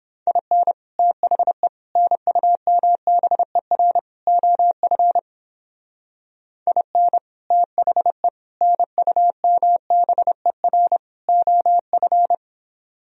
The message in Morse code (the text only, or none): IN THE NUMBER OF IN THE NUMBER OF